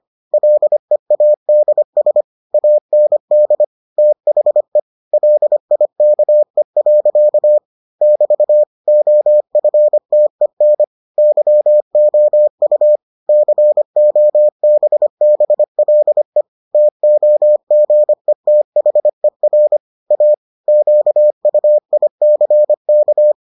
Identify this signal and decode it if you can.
Morse code, heard as a keyed tone, about 25 wpm, 600 Hz. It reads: LEADS AND THE LIKE. <BT> OFTEN YOU COBBLE TOGETHER A QUICK